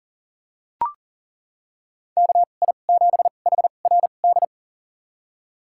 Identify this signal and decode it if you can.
Morse code, heard as a keyed tone, about 40 wpm, spaced as for 29 wpm, 700 Hz. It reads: KI7HRD
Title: USA Call Signs - 6 Characters - 2x Character Spacing - 40wpm